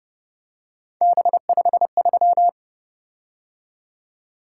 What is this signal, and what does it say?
Morse code, heard as a keyed tone, about 29 wpm, 700 Hz. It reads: B53